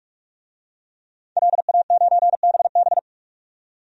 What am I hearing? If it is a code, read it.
Morse code, heard as a keyed tone, about 46 wpm, 700 Hz. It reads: LA9BB
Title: DXCC-100 Call Signs - 5 Characters - 45wpm